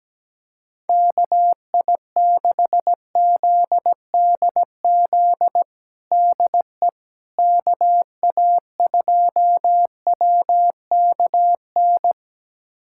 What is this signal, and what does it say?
Morse code, heard as a keyed tone, about 17 wpm, 700 Hz. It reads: KI6ZDZ DE KA2WKN
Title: USA Callsign DE Callsign 17wpm